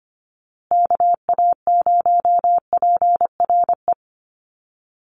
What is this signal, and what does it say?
Morse code, heard as a keyed tone, about 25 wpm, 700 Hz. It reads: KA0PRE
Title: USA Call Signs Repeated 3x 25wpm